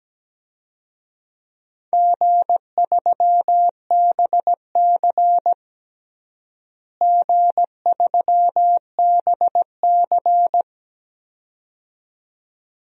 Morse code, heard as a keyed tone, about 17 wpm, 700 Hz. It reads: G3BC G3BC
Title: DXCC-100 Call Signs - 4 Characters - Repeated 2x - 17wpm